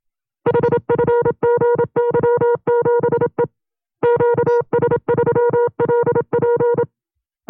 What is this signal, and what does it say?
Morse code, heard as a keyed tone, about 27 wpm, 480 Hz. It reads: HFGY7E QS3LP
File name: FreeFiles / CW-practice / 25WPM-side1